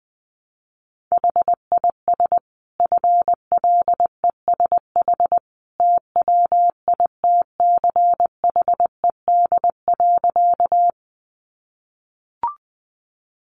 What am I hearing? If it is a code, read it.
Morse code, heard as a keyed tone, about 20 wpm, 700 Hz. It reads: HIS FLESH TWITCHED.
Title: A Sound Of Thunder 20wpm